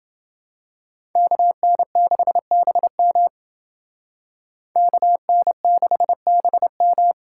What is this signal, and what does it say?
Morse code, heard as a keyed tone, about 29 wpm, 700 Hz. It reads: KN6BM KN6BM